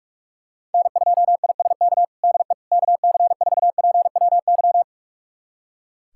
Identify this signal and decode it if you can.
Morse code, heard as a keyed tone, about 46 wpm, 700 Hz. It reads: N2ISK DE KC4PWY
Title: USA Callsign DE Callsign 45wpm